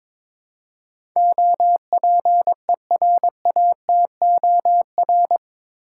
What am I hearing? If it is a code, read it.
Morse code, heard as a keyed tone, about 22 wpm, 700 Hz. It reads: OPERATOR
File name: practice / 22wpm / Eight Letter Words - 22wpm